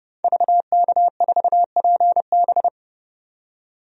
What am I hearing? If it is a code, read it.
Morse code, heard as a keyed tone, about 30 wpm, 700 Hz. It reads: VK4PB
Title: DXCC-100 Call Signs - 5 Characters - 30wpm